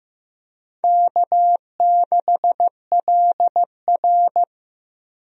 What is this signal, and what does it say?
Morse code, heard as a keyed tone, about 15 wpm, 700 Hz. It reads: K6LR
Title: USA Call Signs - 4 Characters - Repeated 3x - 15wpm